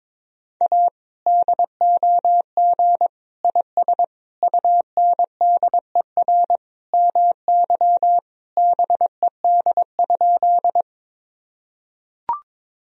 Morse code, heard as a keyed tone, about 22 wpm, 700 Hz. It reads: A DOG IS UNDER MY BED?
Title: Sentences from Top 1000 Words - Review Part 2 - 22wpm